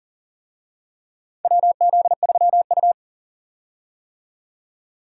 Morse code, heard as a keyed tone, about 40 wpm, 700 Hz. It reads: WZ3U
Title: USA Call Signs - 4 Characters - 40wpm